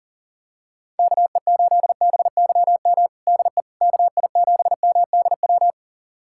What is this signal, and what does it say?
Morse code, heard as a keyed tone, about 40 wpm, 700 Hz. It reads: KE8BYM DE KI7MDW
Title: USA Callsign DE Callsign - 6 Characters - 40wpm